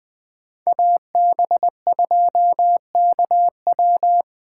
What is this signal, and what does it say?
Morse code, heard as a keyed tone, about 20 wpm, 700 Hz. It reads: AB2KW
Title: Speed Racing - USA 2x2 Call Signs 17wpm